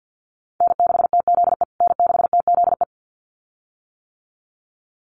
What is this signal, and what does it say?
Morse code, heard as a keyed tone, about 50 wpm, 700 Hz. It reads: N6TZE N6TZE
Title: USA Call Signs Repeated 2x 50wpm